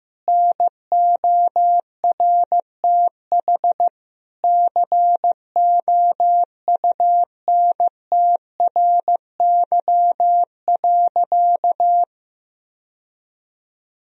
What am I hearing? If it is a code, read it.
Morse code, heard as a keyed tone, about 15 wpm, 700 Hz. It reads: NORTH COUNTRY.
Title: Sentences from Top 1000 Words - Review Part 1 - 15wpm